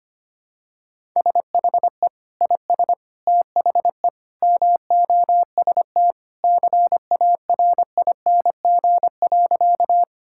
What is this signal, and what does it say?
Morse code, heard as a keyed tone, about 25 wpm, 700 Hz. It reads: SHE IS THE MOST CARING.
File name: practice / 25wpm / Sentences with 2-Word Phrases - 25wpm